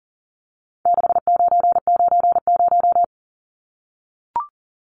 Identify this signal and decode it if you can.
Morse code, heard as a keyed tone, about 40 wpm, 700 Hz. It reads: 6990
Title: Numbers - 4-Digit 40wpm